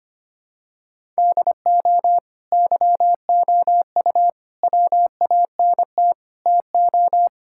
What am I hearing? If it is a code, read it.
Morse code, heard as a keyed tone, about 25 wpm, 700 Hz. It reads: DO YOU WANT TO
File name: practice / 25wpm / Sentences with 2-Word Phrases - 25wpm